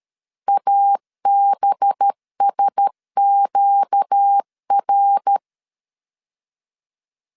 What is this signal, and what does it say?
Morse code, heard as a keyed tone, about 12 wpm, 800 Hz. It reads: ABSQR
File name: LEZIONI / 7 GF9Z / 7 GF9 12wpm 2319